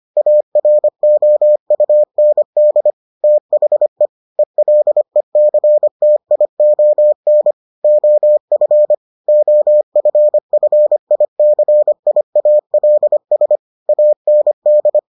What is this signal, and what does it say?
Morse code, heard as a keyed tone, about 25 wpm, 600 Hz. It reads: AROUND THE ELECTION OF OFFICIALS AND